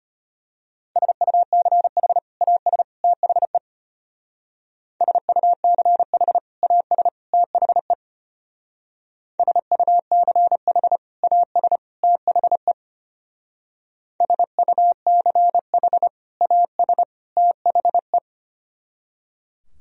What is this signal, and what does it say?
Morse code, heard as a keyed tone, about 32 wpm, 700 Hz. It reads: SUCH AS THE SUCH AS THE SUCH AS THE SUCH AS THE